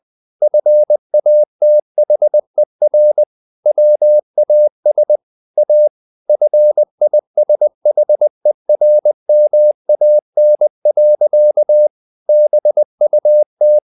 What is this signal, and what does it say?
Morse code, heard as a keyed tone, about 20 wpm, 600 Hz. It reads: FATHER WAS A FISHERMAN. BUT
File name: the old man and the sea / the old man and the sea part 1 20wpm